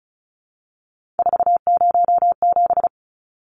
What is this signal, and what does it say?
Morse code, heard as a keyed tone, about 34 wpm, 700 Hz. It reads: 407
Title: Numbers - 3-Digit 35wpm